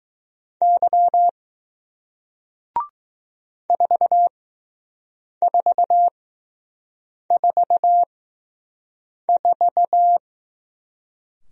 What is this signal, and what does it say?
Morse code, heard as a keyed tone, about 20 wpm, spaced as for 6 wpm, 700 Hz. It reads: Y 4444